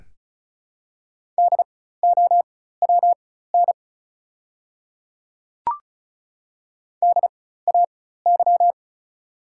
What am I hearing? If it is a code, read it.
Morse code, heard as a keyed tone, about 36 wpm, spaced as for 16 wpm, 700 Hz. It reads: DOWN DAY
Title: 4x Character Spacing - Top 100 Words - 35wpm